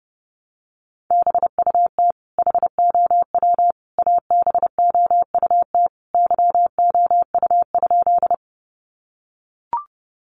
Morse code, heard as a keyed tone, about 30 wpm, 700 Hz. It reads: BUT HOW ABOUT YOU?
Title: Sentences from Top 100 Words 30wpm